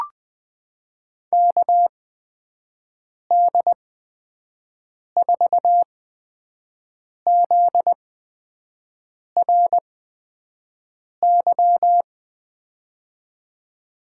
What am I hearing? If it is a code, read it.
Morse code, heard as a keyed tone, about 21 wpm, spaced as for 5 wpm, 700 Hz. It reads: KD4ZRY
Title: USA Call Signs - 6 Characters - 8x Character Spacing - 20wpm